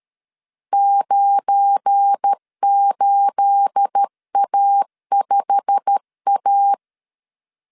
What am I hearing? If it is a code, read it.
Morse code, heard as a keyed tone, about 13 wpm, 800 Hz. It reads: 98A5A
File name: CW / DET / 1 9 1500C S B 12wpm 3819